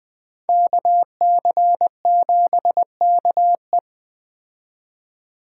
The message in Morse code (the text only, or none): KC7KE